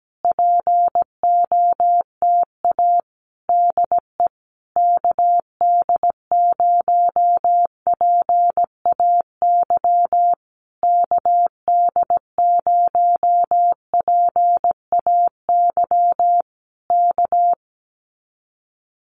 Morse code, heard as a keyed tone, about 17 wpm, 700 Hz. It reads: POTA DE KD0PAY KD0PAY K
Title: Common QSO Phrases Encore 17wpm